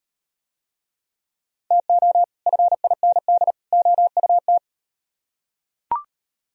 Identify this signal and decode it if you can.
Morse code, heard as a keyed tone, about 36 wpm, 700 Hz. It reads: TO FIND OUT